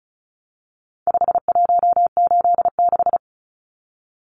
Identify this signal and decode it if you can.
Morse code, heard as a keyed tone, about 34 wpm, 700 Hz. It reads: 5186
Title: Numbers - 4-Digit 35wpm